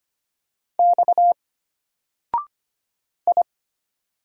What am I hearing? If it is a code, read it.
Morse code, heard as a keyed tone, about 25 wpm, 700 Hz. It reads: X I
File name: Single-Letter-Number-25wpm